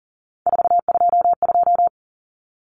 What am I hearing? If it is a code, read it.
Morse code, heard as a keyed tone, about 37 wpm, 700 Hz. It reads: 422
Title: Numbers - 3-Digit 40wpm